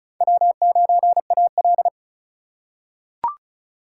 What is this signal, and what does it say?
Morse code, heard as a keyed tone, about 34 wpm, 700 Hz. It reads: W9AL
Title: USA Call Signs - 4 Characters - 3x Character Spacing - 35wpm